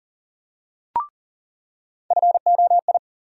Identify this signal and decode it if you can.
Morse code, heard as a keyed tone, about 40 wpm, 700 Hz. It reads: FOI
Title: USA Call Sign Postfixes - 3 Letters 40wpm